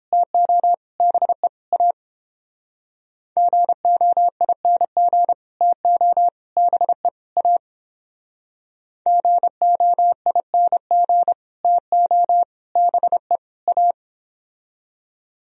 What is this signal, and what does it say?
Morse code, heard as a keyed tone, about 31 wpm, 700 Hz. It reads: TO BE A GOING TO BE A GOING TO BE A